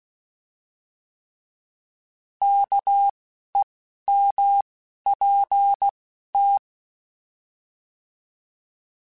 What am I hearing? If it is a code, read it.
Morse code, heard as a keyed tone, about 16 wpm, 800 Hz. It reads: K E M P T
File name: Alphabetical / Alphabetical mp3s / Ks @10